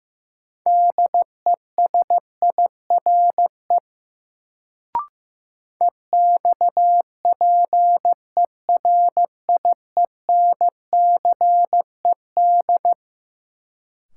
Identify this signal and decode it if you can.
Morse code, heard as a keyed tone, about 15 wpm, 700 Hz. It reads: DESIRE EXPERIENCED
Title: Sentences from Top 1100 Words - New Word Drill - 15wpm